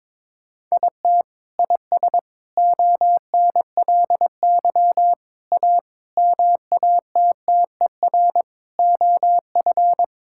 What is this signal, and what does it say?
Morse code, heard as a keyed tone, about 21 wpm, 700 Hz. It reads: IT IS ONLY A MATTER OF
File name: Sets of 9 Words - 22wpm